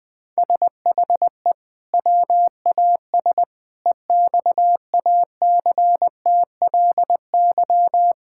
Morse code, heard as a keyed tone, about 21 wpm, 700 Hz. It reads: SHE WAS EXACTLY
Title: Sentences from Top 1000 Words - Review Part 6 - 20wpm